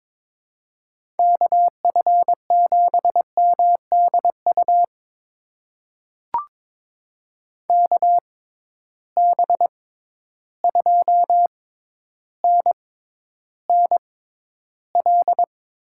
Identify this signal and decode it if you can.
Morse code, heard as a keyed tone, about 22 wpm, spaced as for 11 wpm, 700 Hz. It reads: KF7MDU KB2NNL
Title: USA Call Signs - 6 Characters - 6x Character Spacing - 22wpm